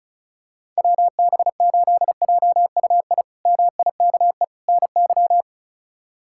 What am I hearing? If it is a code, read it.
Morse code, heard as a keyed tone, about 34 wpm, 700 Hz. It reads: WB8JUI MIKE NY
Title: K1USN SST QSO Practice - 35wpm